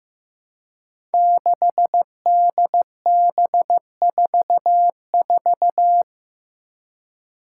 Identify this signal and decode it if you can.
Morse code, heard as a keyed tone, about 15 wpm, 700 Hz. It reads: 6DB44
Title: Five Challenging Letter-Number Combinations 15wpm